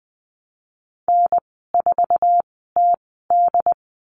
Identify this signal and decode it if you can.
Morse code, heard as a keyed tone, about 20 wpm, spaced as for 15 wpm, 700 Hz. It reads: N4TD
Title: 179 Lesson 29g - Slash - Call Signs 20wpm